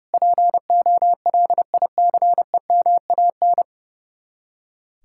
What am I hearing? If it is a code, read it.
Morse code, heard as a keyed tone, about 36 wpm, 700 Hz. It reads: POLICEMAN
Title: Speed-Racing Common QSO Phrases - 30wpm